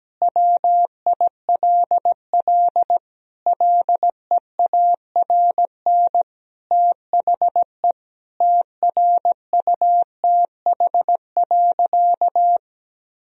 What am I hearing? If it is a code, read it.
Morse code, heard as a keyed tone, about 17 wpm, 700 Hz. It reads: WILL LEARN THE TRUTH.
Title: Sentences from Top 1000 Words - Review Part 4 - 17wpm